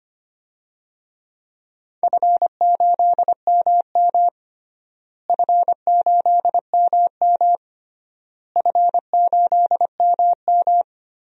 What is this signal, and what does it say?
Morse code, heard as a keyed tone, about 25 wpm, 700 Hz. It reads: F8MM F8MM F8MM
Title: DXCC-100 Call Signs - 4 Characters - Repeated 3x - 25wpm